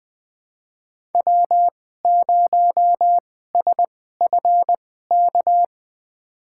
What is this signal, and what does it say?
Morse code, heard as a keyed tone, about 20 wpm, spaced as for 15 wpm, 700 Hz. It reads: W0SFK
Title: USA Call Signs - 5 Characters - 2x Character Spacing - 20wpm